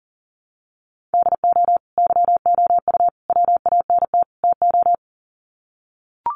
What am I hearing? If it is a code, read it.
Morse code, heard as a keyed tone, about 37 wpm, 700 Hz. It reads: DO YOU WANT TO
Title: Sets of 4 Words 40wpm